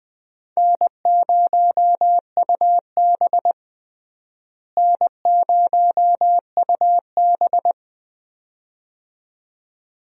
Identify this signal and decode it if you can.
Morse code, heard as a keyed tone, about 20 wpm, 700 Hz. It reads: N0UB N0UB
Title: USA Call Signs - 4 Characters - Repeated 2x - 20wpm